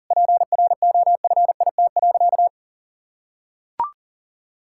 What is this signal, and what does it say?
Morse code, heard as a keyed tone, about 40 wpm, 700 Hz. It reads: PROFIT.
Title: Sentences from Top 1000 Words - Review Part 5 - 40wpm